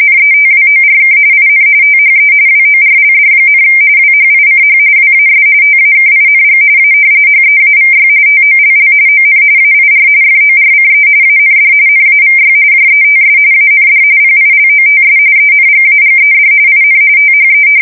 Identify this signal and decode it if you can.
RTTY, about 45.45 baud. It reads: NETWORK IN THE WORLD, WHICH WAS TAKING A HEAVY TOLL ON AIR FORCE AND NAVY JETS AND PILOTS ENGAGED IN THE
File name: fsk170 / battleshipnj